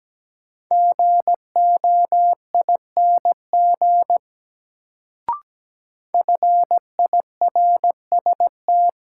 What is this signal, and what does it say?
Morse code, heard as a keyed tone, about 17 wpm, 700 Hz. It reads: GOING FIRST
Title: Sets of 2 Words - New Word Drill - 17wpm